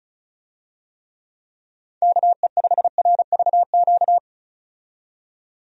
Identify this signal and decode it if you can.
Morse code, heard as a keyed tone, about 36 wpm, 700 Hz. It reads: KE5RVQ
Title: USA Call Signs - 6 Characters - 35wpm